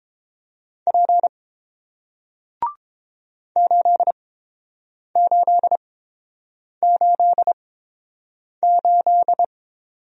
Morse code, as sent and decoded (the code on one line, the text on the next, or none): .--. / ---.. ---.. ---.. ---..
P 8888